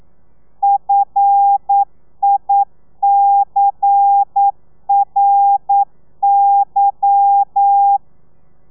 Morse code, heard as a keyed tone, about 9 wpm, 800 Hz. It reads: FICRY